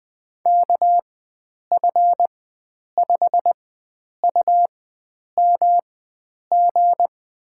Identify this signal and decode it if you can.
Morse code, heard as a keyed tone, about 20 wpm, spaced as for 9 wpm, 700 Hz. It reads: KF5UMG
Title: USA Call Signs - 6 Characters - 4x Character Spacing - 20wpm